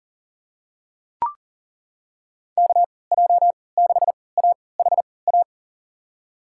Morse code, heard as a keyed tone, about 38 wpm, spaced as for 22 wpm, 700 Hz. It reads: KJ6AHA
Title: USA Call Signs - 6 Characters - 3x Character Spacing - 40wpm